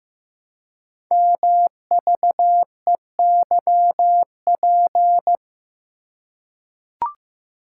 Morse code, heard as a keyed tone, about 15 wpm, 700 Hz. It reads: MVEYP
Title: Code Groups - 5 Letters - 15wpm